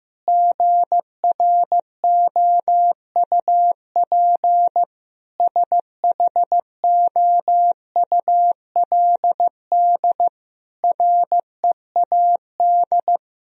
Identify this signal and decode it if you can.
Morse code, heard as a keyed tone, about 15 wpm, 700 Hz. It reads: GROUP SHOULD READ
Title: Sentences from Top 1000 Words - Review Part 1 - 15wpm